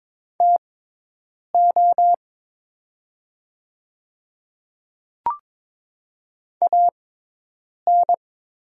Morse code, heard as a keyed tone, about 22 wpm, spaced as for 8 wpm, 700 Hz. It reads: TO AN
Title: 6x Character Spacing - Top 100 Words - 22wpm